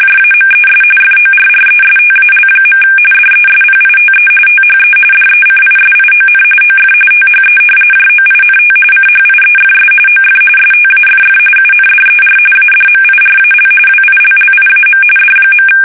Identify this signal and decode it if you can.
RTTY, about 45.45 baud. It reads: JAMES FORRESTAL. DESPITE THE DECISION OF THE JOINT CHIEFS OF STAFF UNDER ADMIRAL WILLIAM D.
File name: fsk850